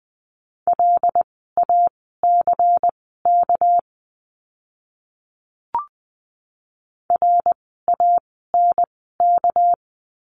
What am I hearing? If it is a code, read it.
Morse code, heard as a keyed tone, about 20 wpm, spaced as for 15 wpm, 700 Hz. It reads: LACK RANK
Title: 199 Lesson 32f - Letter k - Words 20wpm